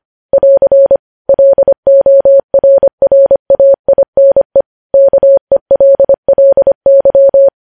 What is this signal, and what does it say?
Morse code, heard as a keyed tone, about 25 wpm, 550 Hz. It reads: <AR> LORRAINE KELLY
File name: podcast / News headlines-1x-25x25